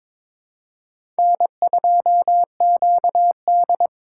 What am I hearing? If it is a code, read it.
Morse code, heard as a keyed tone, about 22 wpm, 700 Hz. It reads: N2QD
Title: USA Call Signs - 3x Character Spacing - 22wpm